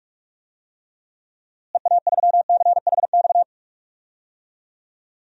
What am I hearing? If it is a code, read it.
Morse code, heard as a keyed tone, about 46 wpm, 700 Hz. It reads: EA3CHX
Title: DXCC-100 Call Signs - 6 Characters - Repeated 2x - 45wpm